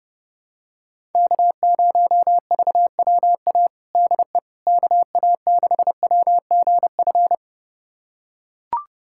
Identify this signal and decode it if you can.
Morse code, heard as a keyed tone, about 30 wpm, 700 Hz. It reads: K0VWA DE KA6WGF
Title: USA Callsign DE Callsign 30wpm